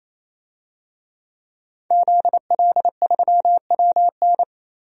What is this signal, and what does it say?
Morse code, heard as a keyed tone, about 28 wpm, 700 Hz. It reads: ZL3WN
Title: DXCC-100 Call Signs - 5 Characters - Repeated 3x - 28wpm